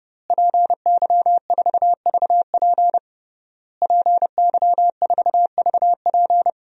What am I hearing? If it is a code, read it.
Morse code, heard as a keyed tone, about 32 wpm, 700 Hz. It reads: PY4VP PY4VP
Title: DXCC-100 Call Signs - 5 Characters - Repeated 3x - 30wpm